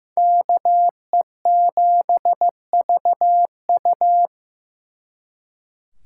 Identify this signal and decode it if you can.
Morse code, heard as a keyed tone, about 18 wpm, 700 Hz. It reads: KE7VU
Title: Speed Racing - USA 2x2 Call Signs 15wpm